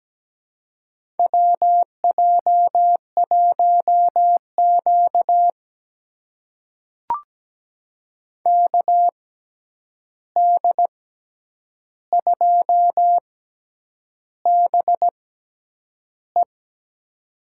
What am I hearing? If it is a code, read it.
Morse code, heard as a keyed tone, about 17 wpm, spaced as for 8 wpm, 700 Hz. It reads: WJ1Q KD2BE
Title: USA Call Signs - 6x Character Spacing - 17wpm